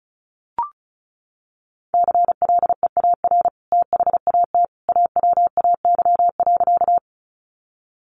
Morse code, heard as a keyed tone, about 34 wpm, 700 Hz. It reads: CLEAR THAT AWAY.
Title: Sentences from Top 600 Words 35wpm